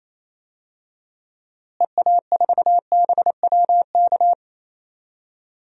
Morse code, heard as a keyed tone, about 28 wpm, 700 Hz. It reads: EA4BWK
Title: DXCC-100 Call Signs - 6 Characters - Repeated 2x - 28wpm